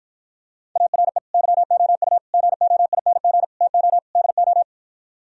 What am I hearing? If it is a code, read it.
Morse code, heard as a keyed tone, about 52 wpm, 700 Hz. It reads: ARE YOU GOING TO DO